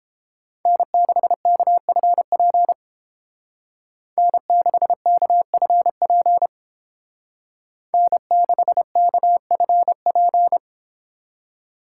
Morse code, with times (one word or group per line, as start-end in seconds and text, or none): 0.7-2.7 N6KFP
4.2-6.5 N6KFP
7.9-10.6 N6KFP